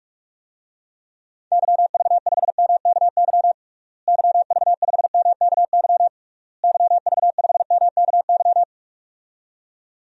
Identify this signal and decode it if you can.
Morse code, heard as a keyed tone, about 44 wpm, 700 Hz. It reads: YV5MKY YV5MKY YV5MKY
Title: DXCC-100 Call Signs - 6 Characters - Repeated 3x - 45wpm